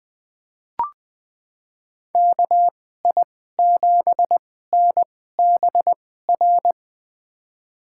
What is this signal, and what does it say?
Morse code, heard as a keyed tone, about 20 wpm, spaced as for 14 wpm, 700 Hz. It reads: KI7NBR
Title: USA Call Signs - 6 Characters - 2x Character Spacing - 20wpm